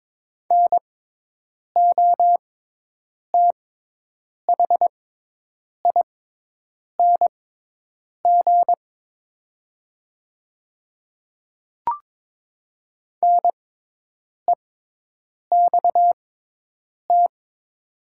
Morse code, heard as a keyed tone, about 22 wpm, spaced as for 8 wpm, 690 Hz. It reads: NOTHING NEXT
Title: 6x Character Spacing - Top 500 Words - 22wpm